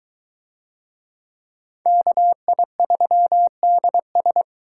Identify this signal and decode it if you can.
Morse code, heard as a keyed tone, about 20 wpm, 700 Hz. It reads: KI3DS